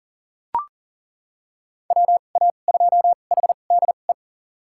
Morse code, heard as a keyed tone, about 38 wpm, spaced as for 28 wpm, 700 Hz. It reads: WA2HDE
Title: USA Call Signs - 6 Characters - 2x Character Spacing - 40wpm